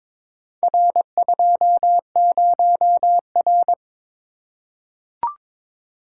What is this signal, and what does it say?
Morse code, heard as a keyed tone, about 22 wpm, 700 Hz. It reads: R20R
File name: practice / 22wpm / Four Letter-Number Combinations - 22wpm